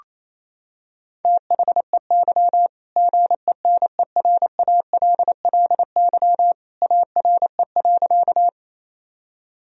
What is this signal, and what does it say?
Morse code, heard as a keyed tone, about 28 wpm, 700 Hz. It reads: THEY GENERALLY ARE.
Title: Sentences from Top 1000 Words - Review Part 4 - 28wpm